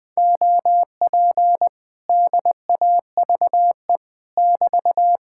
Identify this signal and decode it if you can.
Morse code, heard as a keyed tone, about 20 wpm, 700 Hz. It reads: OP DAVE <BT>
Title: Speed-Racing Common QSO Phrases Encore - 17wpm